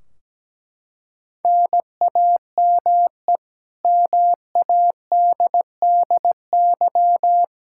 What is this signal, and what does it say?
Morse code, heard as a keyed tone, about 17 wpm, 700 Hz. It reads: NAME MADDY
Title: Practice QSO - Exchange 1 - Beginning - 17wpm